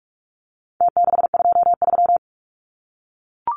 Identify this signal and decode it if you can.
Morse code, heard as a keyed tone, about 44 wpm, 700 Hz. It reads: T623
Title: Numbers - Contest Style - 4-Digit 45wpm